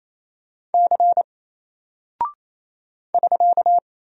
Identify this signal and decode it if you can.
Morse code, heard as a keyed tone, about 27 wpm, 700 Hz. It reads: C <SK>
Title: Single Character (Letter:Number:Punctuation:Prosign) 28wpm